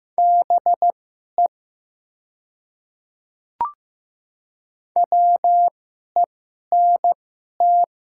Morse code, heard as a keyed tone, about 15 wpm, spaced as for 11 wpm, 700 Hz. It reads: BE WENT
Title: 2x Character Spacing - Top 500 Words - 15wpm